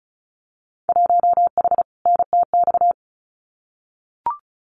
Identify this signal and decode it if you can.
Morse code, heard as a keyed tone, about 34 wpm, 700 Hz. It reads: 1H NTX
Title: Winter Field Day Exchanges 35wpm